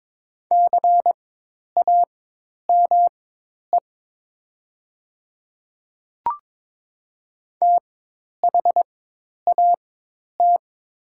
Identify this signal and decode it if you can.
Morse code, heard as a keyed tone, about 22 wpm, spaced as for 10 wpm, 700 Hz. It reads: CAME THAT
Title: 4x Character Spacing - Top 300 Words - 22wpm